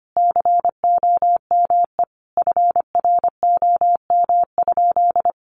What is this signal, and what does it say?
Morse code, heard as a keyed tone, about 25 wpm, 700 Hz. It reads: COME FROM?
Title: Sentences from Top 200 Words 25wpm